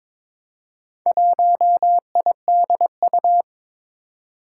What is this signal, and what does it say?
Morse code, heard as a keyed tone, about 22 wpm, 700 Hz. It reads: 1IDU
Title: Four Challenging Letter-Number Combinations 22wpm